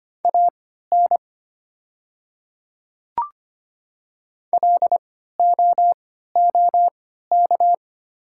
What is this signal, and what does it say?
Morse code, heard as a keyed tone, about 25 wpm, spaced as for 14 wpm, 700 Hz. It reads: AN LOOK
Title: 3x Character Spacing - Top 100 Words - 25wpm